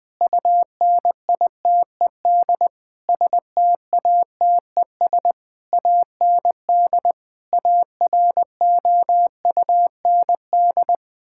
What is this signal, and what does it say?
Morse code, heard as a keyed tone, about 19 wpm, 700 Hz. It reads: UNITED STATES AND AROUND
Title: Sentences with 8-Word Phrases - Spoken Beginning - 20wpm